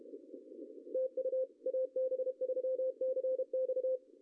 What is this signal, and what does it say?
Morse code, heard as a keyed tone, about 32 wpm, 520 Hz. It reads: TU AB3CX
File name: CW / cqww-cw-k5tr